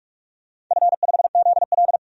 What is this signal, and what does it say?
Morse code, heard as a keyed tone, about 46 wpm, 700 Hz. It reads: F5ZL
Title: DXCC-100 Call Signs - 4 Characters - 45wpm